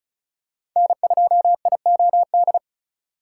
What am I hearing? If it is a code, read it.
Morse code, heard as a keyed tone, about 36 wpm, 700 Hz. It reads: N2IOD
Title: USA Call Signs - 5 Characters - 2x Character Spacing - 35wpm